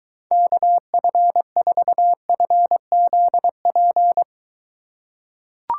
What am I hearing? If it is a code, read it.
Morse code, heard as a keyed tone, about 23 wpm, 700 Hz. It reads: KF4FZP